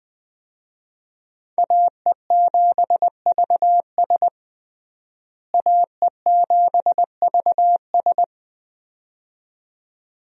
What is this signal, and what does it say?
Morse code, heard as a keyed tone, about 21 wpm, 700 Hz. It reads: AE7VS AE7VS
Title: USA Call Signs - 5 Characters - Repeated 2x - 20wpm